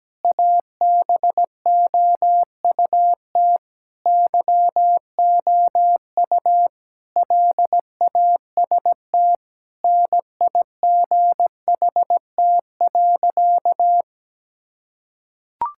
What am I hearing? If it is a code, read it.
Morse code, heard as a keyed tone, about 17 wpm, 700 Hz. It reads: ABOUT YOU LAST NIGHT.
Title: Sentences from Top 1600 Words 17wpm